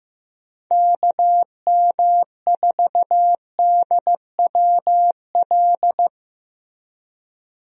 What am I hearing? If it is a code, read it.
Morse code, heard as a keyed tone, about 15 wpm, 700 Hz. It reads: KM4DWL
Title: USA Call Signs - 6 Characters - Repeated 2x - 15wpm